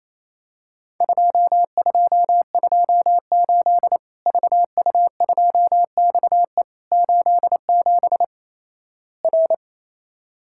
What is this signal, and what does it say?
Morse code, heard as a keyed tone, about 28 wpm, 700 Hz. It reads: 2228 VU2XE 87 R
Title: Worked All Europe DX Contest - QTC Exchange - 28wpm